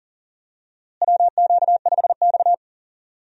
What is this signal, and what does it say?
Morse code, heard as a keyed tone, about 40 wpm, 700 Hz. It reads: WQ5X
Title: USA Call Signs - 4 Characters - Repeated 2x - 40wpm